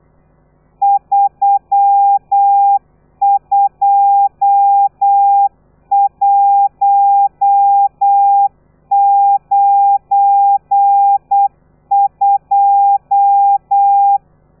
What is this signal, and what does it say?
Morse code, heard as a keyed tone, about 8 wpm, 800 Hz. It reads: 32192
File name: telegrafia / i2tfj / Lez12 35